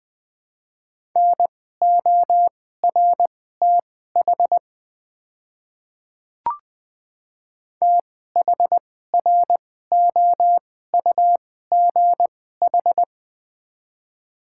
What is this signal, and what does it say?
Morse code, heard as a keyed tone, about 20 wpm, spaced as for 15 wpm, 690 Hz. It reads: NORTH THROUGH
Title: 2x Character Spacing - Top 500 Words - 20wpm